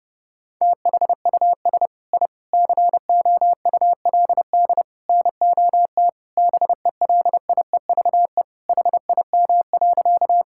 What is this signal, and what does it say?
Morse code, heard as a keyed tone, about 30 wpm, 700 Hz. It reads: THUS I COULD NOT BELIEVE HIM.
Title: Sentences from Top 1000 Words - Review Part 5 - 30wpm